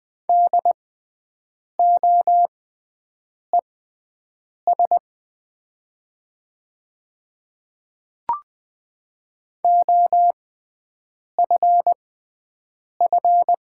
Morse code, heard as a keyed tone, about 20 wpm, spaced as for 7 wpm, 700 Hz. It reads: DOES OFF
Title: 6x Character Spacing - Top 200 Words - 20wpm